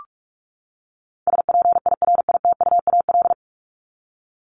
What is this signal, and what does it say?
Morse code, heard as a keyed tone, about 44 wpm, 700 Hz. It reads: SPIRITUAL
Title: Top 1501-1750 Words 45wpm